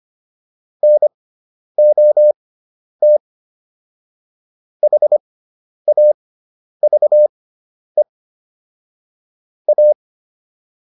Morse code, heard as a keyed tone, about 25 wpm, spaced as for 10 wpm, 600 Hz. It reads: NOT HAVE A